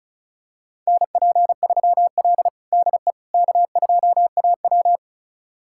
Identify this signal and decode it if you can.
Morse code, heard as a keyed tone, about 36 wpm, 700 Hz. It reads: NP3L DE K2AW
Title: USA Callsign DE Callsign - 4 Characters - 35wpm